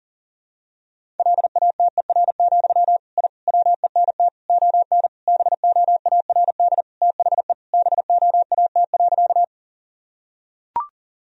Morse code, heard as a keyed tone, about 40 wpm, 700 Hz. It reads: LATER, I WENT ON BOARD THE BOAT.
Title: Sentences from Top 1000 Words - Review Part 4 - 40wpm